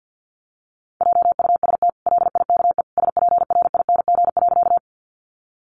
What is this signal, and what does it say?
Morse code, heard as a keyed tone, about 50 wpm, 700 Hz. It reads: JUST LIKE SPRING.
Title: Sentences from Top 1200 Words 50wpm